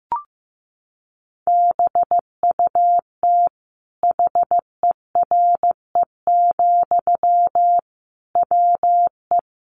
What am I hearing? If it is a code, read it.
Morse code, heard as a keyed tone, about 15 wpm, 700 Hz. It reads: BUT HERE, WE
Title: Sentences from Top 1500 Words 15wpm